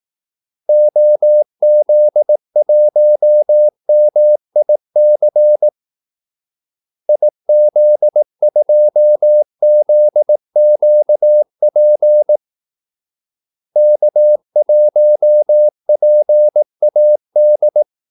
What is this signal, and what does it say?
Morse code, heard as a keyed tone, about 18 wpm, 600 Hz. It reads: OZ1MIC IZ2ZQP K1PAD
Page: Kaldesignaler 18 wpm